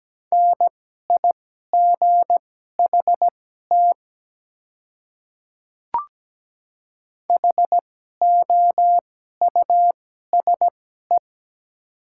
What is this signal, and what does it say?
Morse code, heard as a keyed tone, about 17 wpm, spaced as for 12 wpm, 700 Hz. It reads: NIGHT HOUSE
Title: 2x Character Spacing - Top 300 Words - 17wpm